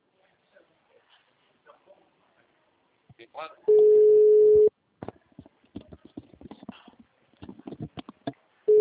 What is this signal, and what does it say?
Ringback tone (Europe, 425 Hz), with speech.